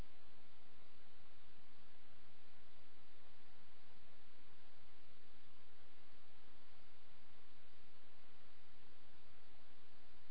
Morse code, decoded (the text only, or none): none